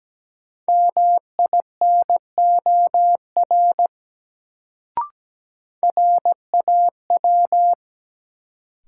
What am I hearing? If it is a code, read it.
Morse code, heard as a keyed tone, about 17 wpm, 700 Hz. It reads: MINOR RAW
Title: Sentences from Top 1500 Words - New Word Drill Alt - 17wpm